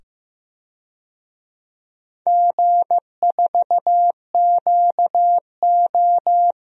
Morse code, heard as a keyed tone, about 15 wpm, 700 Hz. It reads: G4QO
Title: DXCC-100 Call Signs - 4 Characters - 15wpm